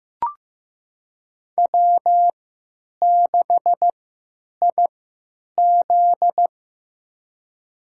Morse code, heard as a keyed tone, about 15 wpm, spaced as for 9 wpm, 700 Hz. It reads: W6IZ
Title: USA Call Signs - 4 Characters - 3x Character Spacing - 15wpm